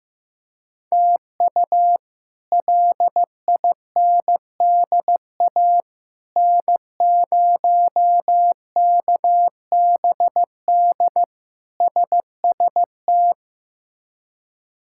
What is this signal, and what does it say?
Morse code, heard as a keyed tone, about 15 wpm, 700 Hz. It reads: TU LINDA N0KBD SST
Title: K1USN SST QSO Practice - 15wpm